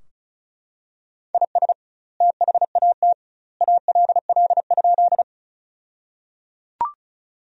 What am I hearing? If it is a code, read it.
Morse code, heard as a keyed tone, about 34 wpm, 700 Hz. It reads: IS THAT ALL?
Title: 2x Word-Spacing - Sentences from Top 100 Words - 35wpm